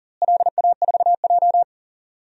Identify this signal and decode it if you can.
Morse code, heard as a keyed tone, about 40 wpm, 700 Hz. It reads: LA4J
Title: DXCC-100 Call Signs - 4 Characters - Repeated 2x - 40wpm